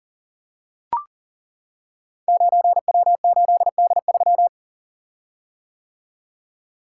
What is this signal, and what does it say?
Morse code, heard as a keyed tone, about 41 wpm, 700 Hz. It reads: 9W8D3
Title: Code Groups - 5 Letter:Numbers - 40wpm